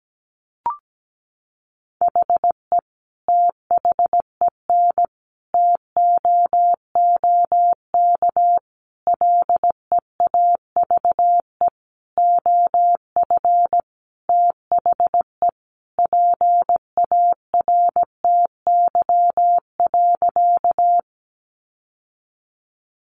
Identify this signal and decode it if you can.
Morse code, heard as a keyed tone, about 17 wpm, 700 Hz. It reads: HE THEN TOOK LEAVE OF THE PARTY.
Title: Sentences from Top 1400 Words 17wpm